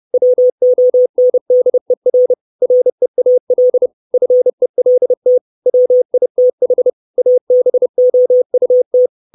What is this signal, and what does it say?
Morse code, heard as a keyed tone, about 30 wpm, 500 Hz. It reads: WONDER REAL FELT WITH ABOUT
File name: 632 words / 632 Words 30WPM